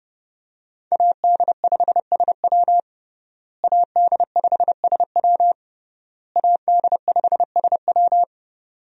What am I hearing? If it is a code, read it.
Morse code, heard as a keyed tone, about 30 wpm, 700 Hz. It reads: AD5SW AD5SW AD5SW
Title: USA Call Signs - 5 Characters - Repeated 3x - 30wpm